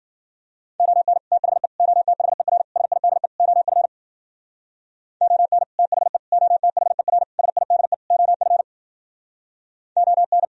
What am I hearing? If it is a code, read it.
Morse code, heard as a keyed tone, about 51 wpm, 700 Hz. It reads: ON THE OTHER SIDE OF ON THE OTHER SIDE OF ON